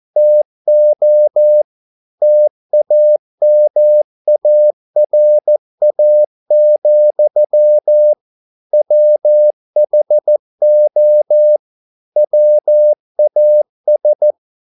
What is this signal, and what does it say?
Morse code, heard as a keyed tone, about 14 wpm, 600 Hz. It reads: TO TAMARA, WHO WAS